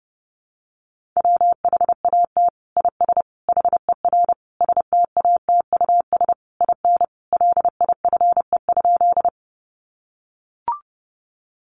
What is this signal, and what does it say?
Morse code, heard as a keyed tone, about 30 wpm, 700 Hz. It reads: WHAT IS HER STATUS IN LIFE?
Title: Sentences from Top 1400 Words 30wpm